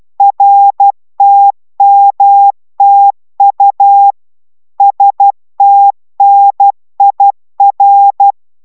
Morse code, heard as a keyed tone, about 12 wpm, 800 Hz. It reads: RTMTU STNIR